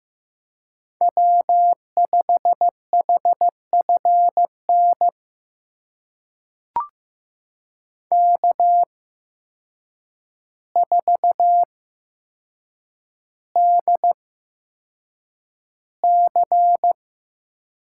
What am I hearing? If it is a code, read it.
Morse code, heard as a keyed tone, about 15 wpm, spaced as for 7 wpm, 700 Hz. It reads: W5HFN K4DC
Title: USA Call Signs - 8x Character Spacing - 15wpm